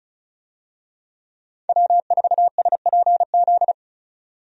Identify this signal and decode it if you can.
Morse code, heard as a keyed tone, about 36 wpm, 700 Hz. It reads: W4SPZ
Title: USA Call Signs - 5 Characters - 35wpm